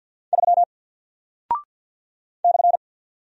Morse code, heard as a keyed tone, about 50 wpm, 700 Hz. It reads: 3 /
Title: Single Character (Letter:Number:Punctuation:Prosign) 50wpm